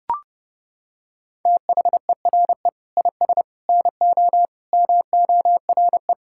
Morse code, heard as a keyed tone, about 32 wpm, 700 Hz. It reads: THERE IS NO MORE
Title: Sentences from Top 1000 Words - Review Part 6 - 30wpm